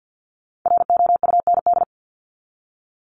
Morse code, heard as a keyed tone, about 50 wpm, 700 Hz. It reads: ROUND
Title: Top 200 Words 50wpm